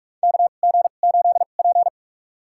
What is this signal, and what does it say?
Morse code, heard as a keyed tone, about 46 wpm, spaced as for 33 wpm, 700 Hz. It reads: KG8P
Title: USA Call Signs - 2x Character Spacing - 45wpm